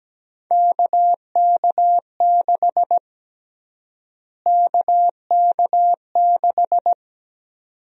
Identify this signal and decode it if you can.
Morse code, heard as a keyed tone, about 17 wpm, 700 Hz. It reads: KK6 KK6
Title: USA Call Sign Prefixes - Repeated 2x - 17wpm